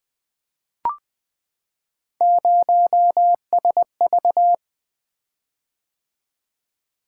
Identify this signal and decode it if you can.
Morse code, heard as a keyed tone, about 20 wpm, 700 Hz. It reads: 0SV
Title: Three Letter-Number Combinations - 20wpm